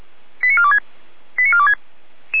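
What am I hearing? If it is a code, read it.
Five-tone selective call (ZVEI), heard as quick runs of five tones, tones of about 70 ms. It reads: 89426 89426
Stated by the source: ZVEI five-tone selective call received on SDR, hosted by Feuerwehr Orlamünde.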